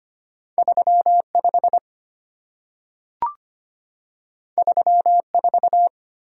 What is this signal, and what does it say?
Morse code, heard as a keyed tone, about 25 wpm, 700 Hz. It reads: 35 34
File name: Two Letter-Number Combinations - 25wpm